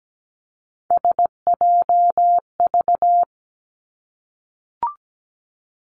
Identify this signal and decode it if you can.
Morse code, heard as a keyed tone, about 17 wpm, 700 Hz. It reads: SJV
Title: ARRL Sections 17wpm